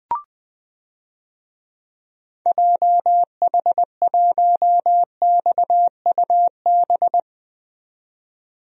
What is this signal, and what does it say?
Morse code, heard as a keyed tone, about 21 wpm, 700 Hz. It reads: JH1XUB
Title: DXCC-100 Call Signs - 6 Characters - 20wpm